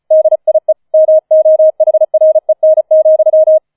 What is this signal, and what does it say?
Morse code, heard as a keyed tone, about 34 wpm, 630 Hz. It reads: DIE MOHREN,
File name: 004 Vierter Streich 35wpm